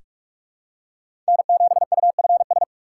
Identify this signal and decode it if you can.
Morse code, heard as a keyed tone, about 48 wpm, 700 Hz. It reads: N7UFS
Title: USA Call Signs - 5 Characters - 6x Character Spacing - 45wpm